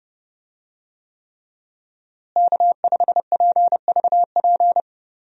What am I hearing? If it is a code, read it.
Morse code, heard as a keyed tone, about 30 wpm, 700 Hz. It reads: K5PVP
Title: USA Call Signs - 5 Characters - Repeated 3x - 30wpm